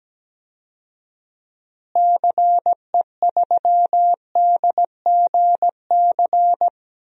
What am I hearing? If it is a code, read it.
Morse code, heard as a keyed tone, about 17 wpm, 700 Hz. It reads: CE3DGC
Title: DXCC-100 Call Signs - 6 Characters - Repeated 2x - 17wpm